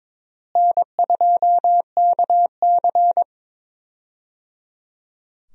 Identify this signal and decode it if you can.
Morse code, heard as a keyed tone, about 21 wpm, 700 Hz. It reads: N2KC
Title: USA Call Signs - 4 Characters - Repeated 3x - 22wpm